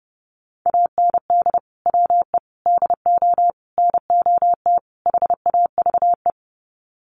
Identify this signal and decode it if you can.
Morse code, heard as a keyed tone, about 28 wpm, 700 Hz. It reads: AND WE DO NOT HAVE
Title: Sets of 5 Words Encore 30wpm